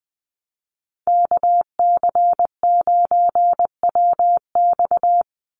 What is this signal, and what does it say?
Morse code, heard as a keyed tone, about 20 wpm, 700 Hz. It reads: KC9WX
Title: USA Call Signs 20wpm